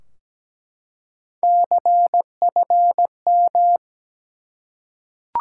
Morse code, heard as a keyed tone, about 17 wpm, 700 Hz. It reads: CFM
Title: Top 90 QSO Elements 17wpm